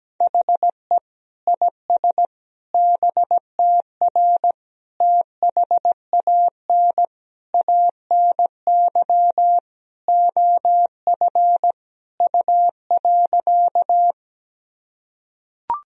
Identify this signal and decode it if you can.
Morse code, heard as a keyed tone, about 17 wpm, 700 Hz. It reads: HE IS BTR THAN ANY OF U.
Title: Sentences from Top 400 Words and Abbreviations 17wpm